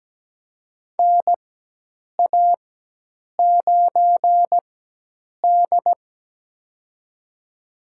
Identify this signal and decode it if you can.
Morse code, heard as a keyed tone, about 17 wpm, spaced as for 8 wpm, 700 Hz. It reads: NA9D
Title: USA Call Signs - 4 Characters - 4x Character Spacing - 17wpm